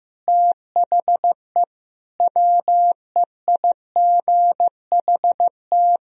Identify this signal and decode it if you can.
Morse code, heard as a keyed tone, about 15 wpm, 700 Hz. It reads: THE WEIGHT
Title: Sentences from Top 1000 Words - Review Part 3 - 15wpm